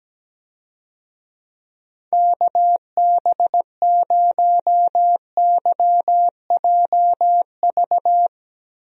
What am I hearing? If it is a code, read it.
Morse code, heard as a keyed tone, about 17 wpm, 700 Hz. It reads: KB0YJV
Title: USA Call Signs - 6 Characters - Repeated 2x - 17wpm